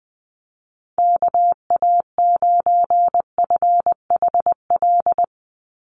Morse code, heard as a keyed tone, about 20 wpm, 700 Hz. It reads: KA9FHL
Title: USA Call Signs 20wpm